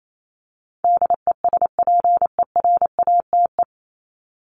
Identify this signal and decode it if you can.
Morse code, heard as a keyed tone, about 28 wpm, 700 Hz. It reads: DESPERATE
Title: Top 1751-2000 Words 28wpm